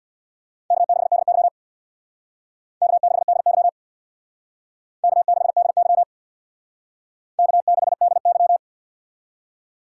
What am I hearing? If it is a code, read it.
Morse code, heard as a keyed tone, about 61 wpm, 700 Hz. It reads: K6DY K6DY K6DY K6DY
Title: Speed Racing - USA 1x2 Call Signs 50wpm